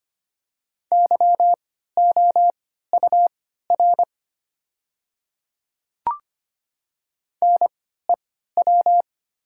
Morse code, heard as a keyed tone, about 25 wpm, spaced as for 14 wpm, 700 Hz. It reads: YOUR NEW
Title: 3x Character Spacing - Top 300 Words - 25wpm